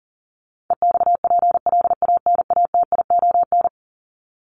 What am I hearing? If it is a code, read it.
Morse code, heard as a keyed tone, about 40 wpm, 700 Hz. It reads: EXPLANATION
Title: Top 1751-2000 Words 40wpm